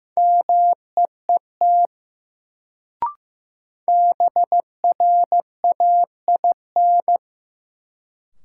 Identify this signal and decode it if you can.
Morse code, heard as a keyed tone, about 15 wpm, 700 Hz. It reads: MEET BRAIN
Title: Sentences from Top 1000 Words - Review Part 6 - New Word Drill - 15wpm